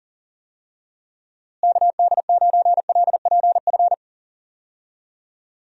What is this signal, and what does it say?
Morse code, heard as a keyed tone, about 40 wpm, 700 Hz. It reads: KD9LPF
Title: USA Call Signs - 6 Characters - Repeated 2x - 40wpm